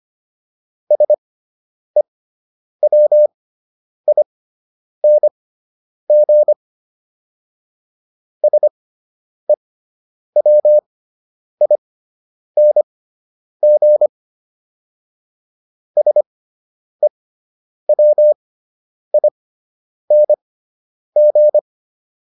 Morse code, heard as a keyed tone, about 25 wpm, spaced as for 9 wpm, 600 Hz. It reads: SEWING SEWING SEWING
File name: ING6 09